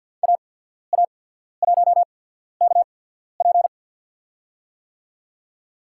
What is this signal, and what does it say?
Morse code, heard as a keyed tone, about 53 wpm, spaced as for 14 wpm, 700 Hz. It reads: AA1KP
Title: USA Call Signs - 5 Characters - 8x Character Spacing - 50wpm